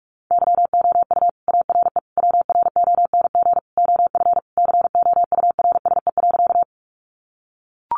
Morse code, heard as a keyed tone, about 44 wpm, 700 Hz. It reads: YOU ARE WRONG OF COURSE.
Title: Sentences from Top 900 Words 45wpm